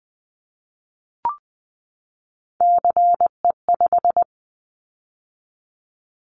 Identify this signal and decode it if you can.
Morse code, heard as a keyed tone, about 19 wpm, 700 Hz. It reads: CE5
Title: Callsign Prefixes for Top 100 DXCC - 20wpm